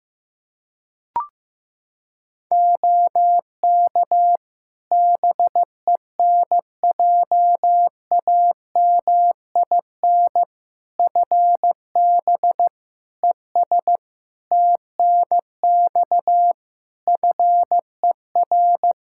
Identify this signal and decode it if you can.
Morse code, heard as a keyed tone, about 16 wpm, 700 Hz. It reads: OK BENJAMIN FB ES TNX FER
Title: K1USN SST QSO Practice 15wpm